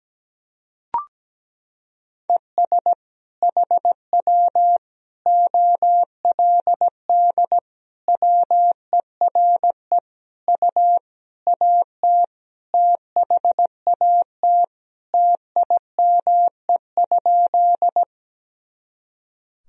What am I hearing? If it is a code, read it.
Morse code, heard as a keyed tone, about 17 wpm, 700 Hz. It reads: ES HW OLD WERE U AT THAT TIME?
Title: Sentences from Top 200 Words and Abbreviations 17wpm